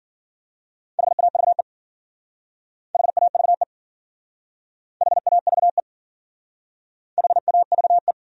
Morse code, heard as a keyed tone, about 52 wpm, 700 Hz. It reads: HAVE HAVE HAVE HAVE